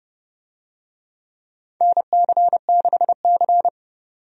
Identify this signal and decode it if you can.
Morse code, heard as a keyed tone, about 30 wpm, 700 Hz. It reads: NC6C
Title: USA Call Signs - 4 Characters - 30wpm